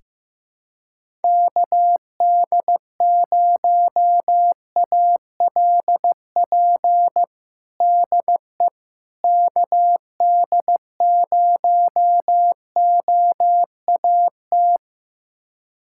Morse code, heard as a keyed tone, about 15 wpm, 700 Hz. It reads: KD0ALP DE KD0OAT
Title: USA Callsign DE Callsign 15wpm